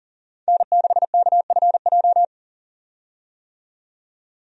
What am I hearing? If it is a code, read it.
Morse code, heard as a keyed tone, about 40 wpm, 700 Hz. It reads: N6KFJ
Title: USA Call Signs - 5 Characters - 40wpm